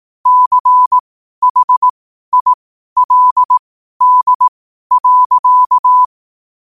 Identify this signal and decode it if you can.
Morse code, heard as a keyed tone, about 18 wpm, spaced as for 12 wpm, 1000 Hz. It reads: CHILD.
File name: qotdfiles / 12wpm